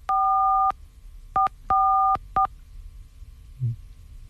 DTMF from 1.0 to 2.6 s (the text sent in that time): 444